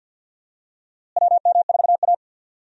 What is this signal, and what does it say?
Morse code, heard as a keyed tone, about 52 wpm, 700 Hz. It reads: WM4A
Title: USA Call Signs - 4 Characters - Repeated 3x - 50wpm